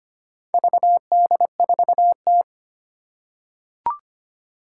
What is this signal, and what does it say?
Morse code, heard as a keyed tone, about 25 wpm, 700 Hz. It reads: VD4T
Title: Four Challenging Letter-Number Combinations 25wpm